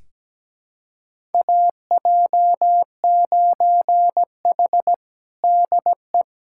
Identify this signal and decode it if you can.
Morse code, heard as a keyed tone, about 17 wpm, 700 Hz. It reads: AJ9H DE
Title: USA Callsign DE Callsign - 4 Characters - 17wpm